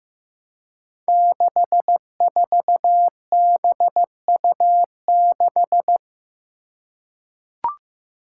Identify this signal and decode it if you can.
Morse code, heard as a keyed tone, about 16 wpm, 700 Hz. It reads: 64BU6
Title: Five Challenging Letter-Number Combinations 15wpm